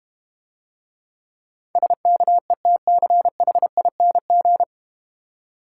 Morse code, heard as a keyed tone, about 31 wpm, 700 Hz. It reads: SKETCHING
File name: Easy Words - Speed-Racing with Hints - 25wpm